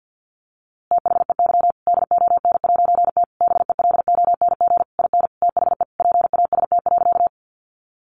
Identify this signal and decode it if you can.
Morse code, heard as a keyed tone, about 50 wpm, 700 Hz. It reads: THEY DON'T BELONG IN THE PAST.
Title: A Sound Of Thunder 50wpm